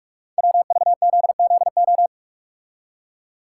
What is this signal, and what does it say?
Morse code, heard as a keyed tone, about 44 wpm, 690 Hz. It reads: WVZZO
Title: Code Groups - 5 Letters - 45wpm